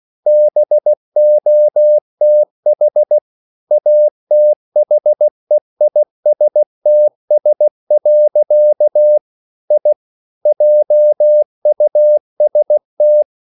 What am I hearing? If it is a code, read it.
Morse code, heard as a keyed tone, about 16 wpm, 600 Hz. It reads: BOTH ATHEISTS. I JUST
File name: CW / Practice / EnglishQuotations0416